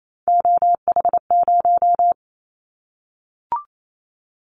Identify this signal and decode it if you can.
Morse code, heard as a keyed tone, about 28 wpm, 700 Hz. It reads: OH0
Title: Callsign Prefixes for Top 100 DXCC - 28wpm